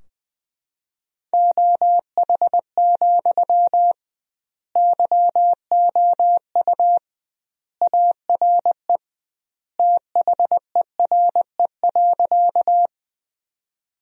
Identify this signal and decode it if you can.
Morse code, heard as a keyed tone, about 20 wpm, 700 Hz. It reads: OH, YOU ARE THERE.
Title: 2x Word-Spacing - Sentences from Top 500 Words - 20wpm